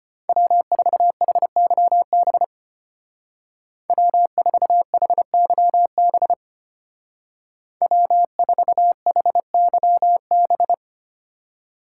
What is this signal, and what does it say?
Morse code, heard as a keyed tone, about 30 wpm, 700 Hz. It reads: W4HYB W4HYB W4HYB
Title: Speed Racing - USA 1x3 Call Signs 25wpm